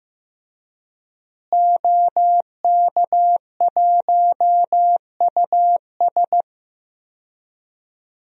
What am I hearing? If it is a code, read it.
Morse code, heard as a keyed tone, about 15 wpm, 700 Hz. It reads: OK1US
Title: DXCC-100 Call Signs - 5 Characters - Repeated 3x - 15wpm